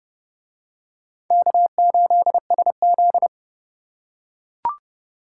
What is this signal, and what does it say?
Morse code, heard as a keyed tone, about 31 wpm, 700 Hz. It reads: K8SZ